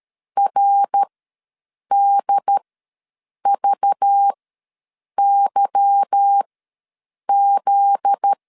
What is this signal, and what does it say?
Morse code, heard as a keyed tone, about 13 wpm, spaced as for 7 wpm, 800 Hz. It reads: RDVYZ
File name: LEZIONI / 9 286 ws 12wpm 3351